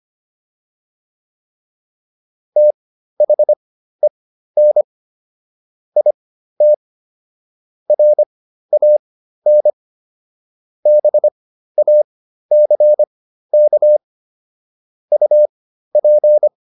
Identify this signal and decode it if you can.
Morse code, heard as a keyed tone, about 25 wpm, spaced as for 13 wpm, 600 Hz. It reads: THEN IT RAN BACK UP